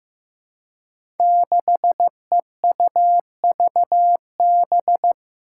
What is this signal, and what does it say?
Morse code, heard as a keyed tone, about 15 wpm, 700 Hz. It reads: 6EUVB
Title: Five Challenging Letter-Number Combinations 15wpm